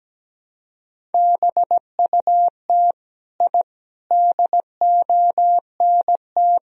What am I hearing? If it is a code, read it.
Morse code, heard as a keyed tone, about 17 wpm, 700 Hz. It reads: BUT I DONT
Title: Sentences from Top 1000 Words - Review Part 1 - 17wpm